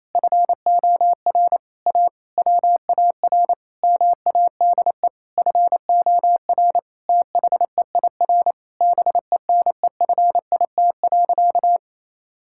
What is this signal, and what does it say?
Morse code, heard as a keyed tone, about 28 wpm, 700 Hz. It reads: FOR A WAR MADE FOR THEIR BENEFIT.
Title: Sentences from Top 1000 Words - Review Part 6 - 28wpm